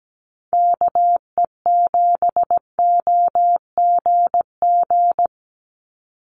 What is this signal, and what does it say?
Morse code, heard as a keyed tone, about 17 wpm, 700 Hz. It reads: KE7OGG
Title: USA Call Signs 17wpm